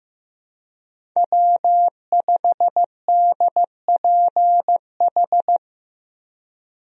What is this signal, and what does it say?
Morse code, heard as a keyed tone, about 15 wpm, 700 Hz. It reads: W5DPH
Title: USA Call Signs - 5 Characters - Repeated 2x - 15wpm